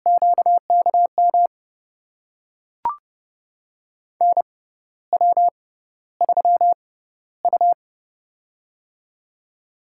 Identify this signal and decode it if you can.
Morse code, heard as a keyed tone, about 32 wpm, spaced as for 14 wpm, 700 Hz. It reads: QKM NW3U
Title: USA Call Signs - 6x Character Spacing - 30wpm